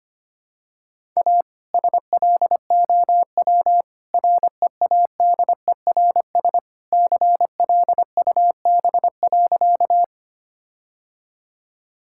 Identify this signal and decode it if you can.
Morse code, heard as a keyed tone, about 25 wpm, 700 Hz. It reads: A SLOW READERS CLUB.
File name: practice / 25wpm / All-Age Appropriate Jokes - 25wpm